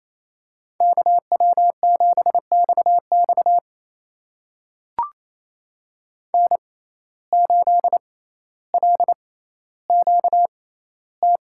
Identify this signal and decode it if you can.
Morse code, heard as a keyed tone, about 27 wpm, spaced as for 14 wpm, 700 Hz. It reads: KW7XX N8LQT
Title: USA Call Signs - 5 Characters - 6x Character Spacing - 28wpm